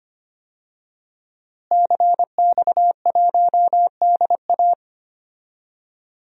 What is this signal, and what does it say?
Morse code, heard as a keyed tone, about 25 wpm, 700 Hz. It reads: CX1DA
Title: DXCC-100 Call Signs - 5 Characters - Repeated 3x - 25wpm